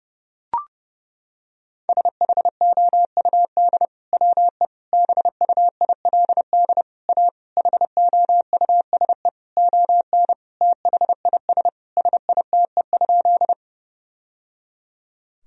Morse code, heard as a keyed tone, about 30 wpm, 700 Hz. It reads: SHOUD WE BUILD A HOUSE ON THIS SITE?
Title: Sentences from Top 1000 Words - Review Part 4 - 30wpm